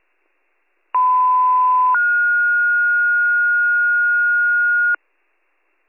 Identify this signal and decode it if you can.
Two-tone page, heard as a one-second tone, then a three-second tone, about 1010 then 1430 Hz, with noise. It is heard from 0.9 to 5.0 s.